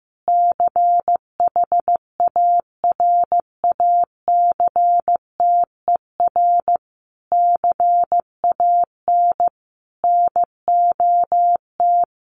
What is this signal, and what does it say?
Morse code, heard as a keyed tone, about 15 wpm, 700 Hz. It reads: CHARACTER CAN NOT
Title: Sets of 5 Words Encore2 15wpm